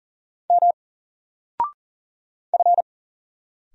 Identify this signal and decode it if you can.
Morse code, heard as a keyed tone, about 44 wpm, 700 Hz. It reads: M F
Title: Single Letter-Number 40wpm